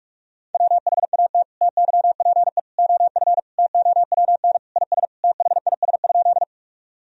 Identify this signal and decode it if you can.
Morse code, heard as a keyed tone, about 47 wpm, 700 Hz. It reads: WHAT TYPE OF TOWN IS THIS?
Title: Sentences from Top 1000 Words - Review Part 3 - 45wpm